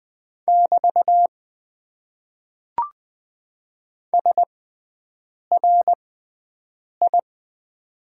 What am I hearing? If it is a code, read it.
Morse code, heard as a keyed tone, about 20 wpm, spaced as for 7 wpm, 700 Hz. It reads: <BT> SRI
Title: Top 90 QSO Elements - 6x Character Spacing - 20wpm